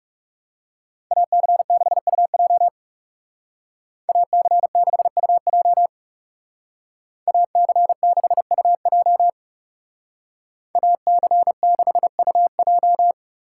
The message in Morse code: AC6UJ AC6UJ AC6UJ AC6UJ

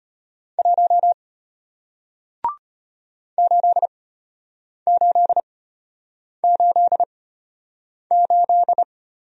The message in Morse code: 1 8888

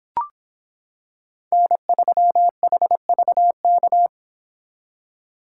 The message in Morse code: N3HVK

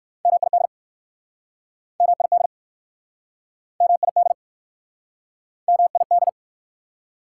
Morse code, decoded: MID MID MID MID

